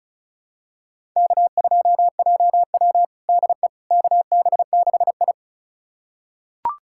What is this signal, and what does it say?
Morse code, heard as a keyed tone, about 34 wpm, 700 Hz. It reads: K2JW DE KB6I
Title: USA Callsign DE Callsign - 4 Characters - 35wpm